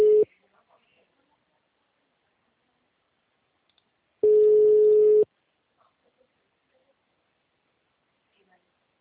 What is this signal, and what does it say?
Ringback tone (Europe, 425 Hz), with speech.